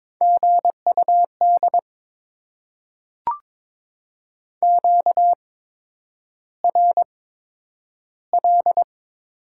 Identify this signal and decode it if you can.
Morse code, heard as a keyed tone, about 21 wpm, spaced as for 9 wpm, 700 Hz. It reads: GUD QRL
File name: Top 20 QSO Elements - 8x Character Spacing - 22wpm